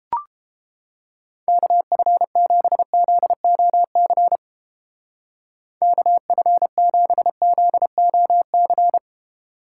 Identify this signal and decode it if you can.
Morse code, heard as a keyed tone, about 29 wpm, 700 Hz. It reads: KF7ZOC KF7ZOC